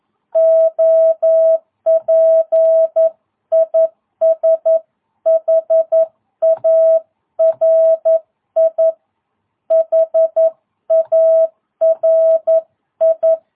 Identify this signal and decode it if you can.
Morse code, heard as a keyed tone, about 11 wpm, 650 Hz. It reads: OP IS HARI HARI
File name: jh8cbh / 2022cw